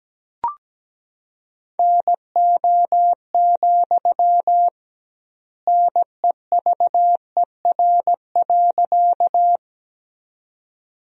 Morse code, heard as a keyed tone, about 17 wpm, 700 Hz. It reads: NO, NEVER.
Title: 2x Word-Spacing - Sentences from Top 300 Words - 17wpm